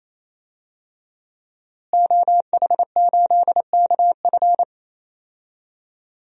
Morse code, heard as a keyed tone, about 28 wpm, 700 Hz. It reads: OH8KF
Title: DXCC-100 Call Signs - 5 Characters - Repeated 3x - 28wpm